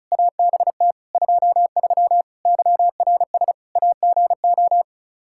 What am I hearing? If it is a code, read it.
Morse code, heard as a keyed tone, about 36 wpm, 700 Hz. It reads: ABT 23 YRS AGO
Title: K1USN SST QSO Practice 35wpm